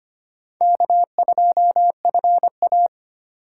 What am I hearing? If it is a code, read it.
Morse code, heard as a keyed tone, about 25 wpm, 700 Hz. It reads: K2FA
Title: USA Call Signs - 4 Characters - 3x Character Spacing - 25wpm